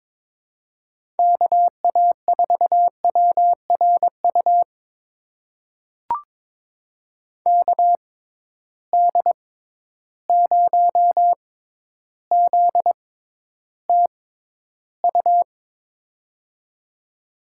Morse code, heard as a keyed tone, about 22 wpm, spaced as for 11 wpm, 700 Hz. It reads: KA4WRU KD0ZTU
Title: USA Call Signs - 6 Characters - 6x Character Spacing - 22wpm